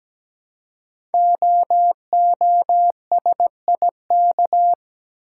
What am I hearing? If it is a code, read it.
Morse code, heard as a keyed tone, about 17 wpm, 700 Hz. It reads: OOSIK
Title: Code Groups - 5 Letters - 17wpm